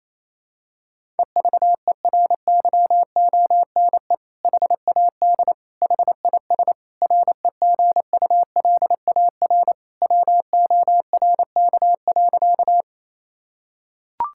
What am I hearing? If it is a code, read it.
Morse code, heard as a keyed tone, about 28 wpm, 700 Hz. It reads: EVERYONE HAD HIS REGULAR WORK.
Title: Sentences from Top 1000 Words - Review Part 6 - 28wpm